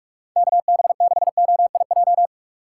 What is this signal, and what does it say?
Morse code, heard as a keyed tone, about 46 wpm, 700 Hz. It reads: KB6OIJ
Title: USA Call Signs - 6 Characters - 6x Character Spacing - 45wpm